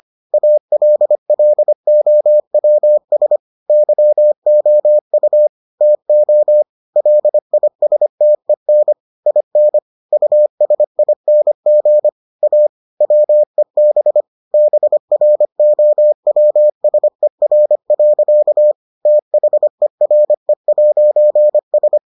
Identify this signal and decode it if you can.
Morse code, heard as a keyed tone, about 25 wpm, 600 Hz. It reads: ALLOWS YOU TO LISTEN IN USING A WEB BROWSER. THERE'S